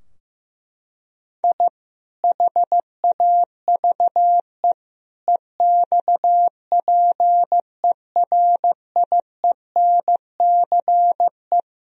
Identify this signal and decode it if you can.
Morse code, heard as a keyed tone, about 15 wpm, 700 Hz. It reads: I HAVE EXPERIENCE
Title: Sentences from Top 1000 Words - Review Part 3 - 15wpm